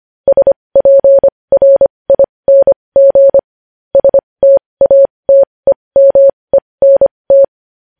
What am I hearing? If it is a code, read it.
Morse code, heard as a keyed tone, about 25 wpm, spaced as for 20 wpm, 550 Hz. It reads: SPRING STATEMENT
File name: podcast / News headlines-1x-25x20